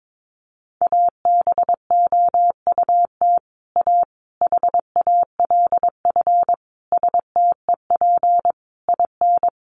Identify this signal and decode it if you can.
Morse code, heard as a keyed tone, about 22 wpm, 700 Hz. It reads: ABOUT A HALF STEP IN
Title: Sentences from Top 1400 Words 22wpm